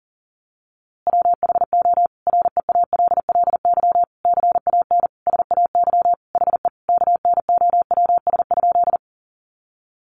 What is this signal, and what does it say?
Morse code, heard as a keyed tone, about 40 wpm, 700 Hz. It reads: WHO REALLY CAN SAY HE KNOWS?
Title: A Sound Of Thunder 40wpm